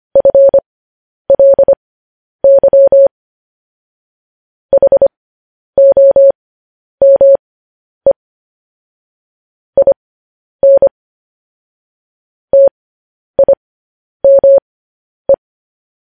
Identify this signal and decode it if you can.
Morse code, heard as a keyed tone, about 25 wpm, spaced as for 10 wpm, 550 Hz. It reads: FLY HOME IN TIME
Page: News Headlines in Morse Code at 10 WPM